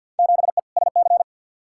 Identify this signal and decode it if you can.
Morse code, heard as a keyed tone, about 47 wpm, 700 Hz. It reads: 7E SC
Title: Field Day Exchanges Encore - 50wpm